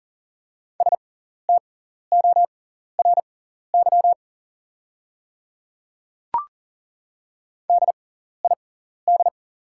Morse code, heard as a keyed tone, about 41 wpm, spaced as for 14 wpm, 690 Hz. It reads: STORY DID
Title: 6x Character Spacing - Top 300 Words - 40wpm